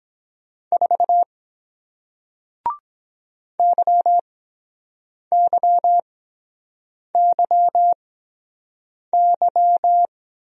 Morse code, heard as a keyed tone, about 23 wpm, spaced as for 7 wpm, 700 Hz. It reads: 4 YYYY